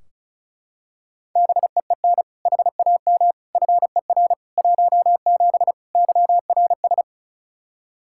Morse code, heard as a keyed tone, about 36 wpm, 700 Hz. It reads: BEEN HAM FER 17 YRS
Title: Practice QSO - Exchange 3 - More Info - 35wpm